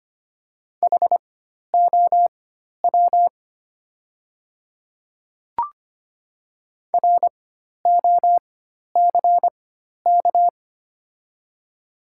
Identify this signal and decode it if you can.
Morse code, heard as a keyed tone, about 25 wpm, spaced as for 12 wpm, 700 Hz. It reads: HOW ROCK
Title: 4x Character Spacing - Top 500 Words - 25wpm